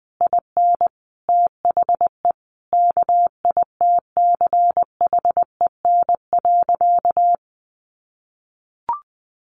Morse code, heard as a keyed tone, about 20 wpm, 700 Hz. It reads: IN THE KITCHEN.
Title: Sentences from Top 1400 Words 20wpm